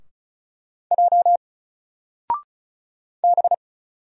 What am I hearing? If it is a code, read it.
Morse code, heard as a keyed tone, about 37 wpm, 700 Hz. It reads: J B
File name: Single-Letter-Number-35wpm